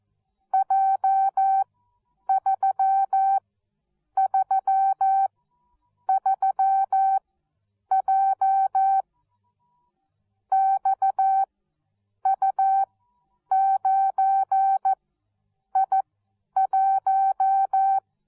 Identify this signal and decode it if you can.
Morse code, heard as a keyed tone, about 14 wpm, spaced as for 8 wpm, 770 Hz. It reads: J333J XU9I1
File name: Onlinelehrgang / morsen / DJ4UF Morsen13